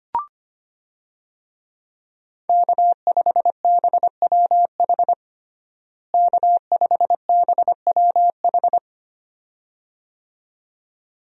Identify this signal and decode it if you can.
Morse code, heard as a keyed tone, about 25 wpm, 700 Hz. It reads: K5BWH K5BWH
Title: USA Call Signs - 5 Characters - Repeated 2x - 25wpm